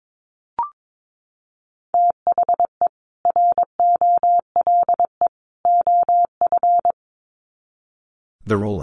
Morse code, heard as a keyed tone, about 22 wpm, 700 Hz. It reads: THE ROLE OF
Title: Sets of 3 Words 22wpm